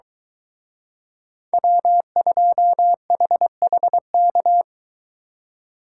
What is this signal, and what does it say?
Morse code, heard as a keyed tone, about 23 wpm, 700 Hz. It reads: W2HHK